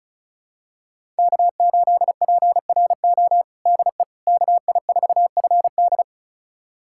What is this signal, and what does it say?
Morse code, heard as a keyed tone, about 36 wpm, 700 Hz. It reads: K8PRO DE KI4FD
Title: USA Callsign DE Callsign - 5 Characters - 35wpm